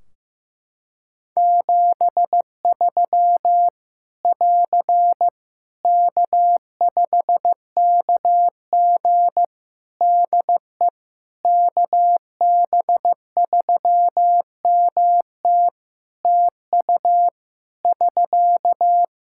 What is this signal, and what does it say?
Morse code, heard as a keyed tone, about 16 wpm, 700 Hz. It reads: 73 <AR> K5KG DE KB3MT TU <SK>
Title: K1USN SST QSO Practice 15wpm